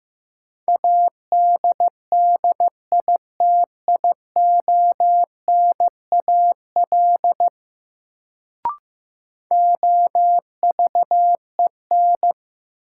Morse code, heard as a keyed tone, about 15 wpm, 700 Hz. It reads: ADDITIONAL OVEN
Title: Sentences from Top 1000 Words - Review Part 4 - New Word Drill - 15wpm